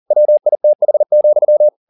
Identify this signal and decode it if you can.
Morse code, heard as a keyed tone, about 40 wpm, 600 Hz. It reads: WITH,
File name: the old man and the sea 2 40wpm